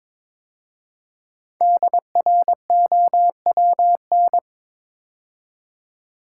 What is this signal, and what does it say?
Morse code, heard as a keyed tone, about 22 wpm, 700 Hz. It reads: DROWN
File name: 5 Letter Words - First 3 Letters Given - 22wpm